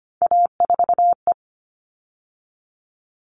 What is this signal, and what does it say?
Morse code, heard as a keyed tone, about 25 wpm, 700 Hz. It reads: A4E
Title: All Cut Numbers - 3-Digit 25wpm